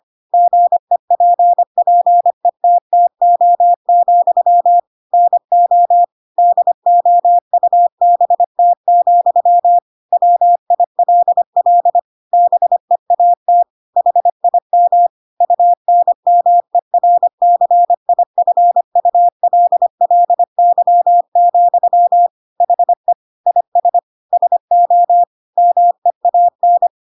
Morse code, heard as a keyed tone, about 25 wpm, 700 Hz. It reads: GEPPETTO, NO DOUBT, WILL BEAT HIM UNMERCIFULLY, HE IS SO MEAN
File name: Pinocchio / Pinocchio ch3 25 25 ik1hge